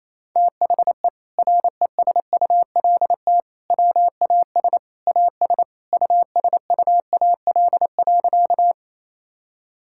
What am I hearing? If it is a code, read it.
Morse code, heard as a keyed tone, about 28 wpm, 700 Hz. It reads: THE RESULT WAS AS USUAL.
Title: Sentences from Top 1000 Words - Review Part 5 - 28wpm